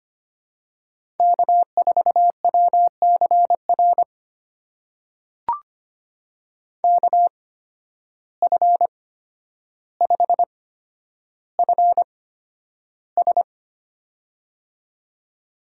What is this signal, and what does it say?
Morse code, heard as a keyed tone, about 25 wpm, spaced as for 11 wpm, 700 Hz. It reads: K4WCR KF5FS
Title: USA Call Signs - 5 Characters - 8x Character Spacing - 25wpm